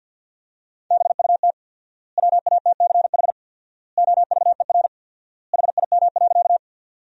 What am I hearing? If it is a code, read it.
Morse code, heard as a keyed tone, about 50 wpm, 700 Hz. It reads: BUT WATCH OVER HIM.
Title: 4x Word-Spacing - Sentences from Top 300 Words - 50wpm